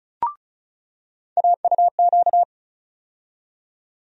Morse code, heard as a keyed tone, about 36 wpm, 700 Hz. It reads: AUQ
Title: USA Call Sign Postfixes - 3 Letters 35wpm